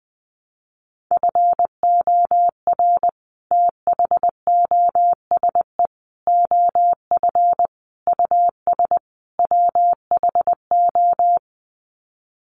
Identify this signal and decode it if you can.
Morse code, heard as a keyed tone, about 19 wpm, 700 Hz. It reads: FOR THOSE OF US WHO
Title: Sets of 5 Words Encore 20wpm